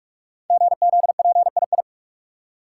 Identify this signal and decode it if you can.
Morse code, heard as a keyed tone, about 41 wpm, 700 Hz. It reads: GZPII
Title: Code Groups - 5 Letter:Numbers - 45wpm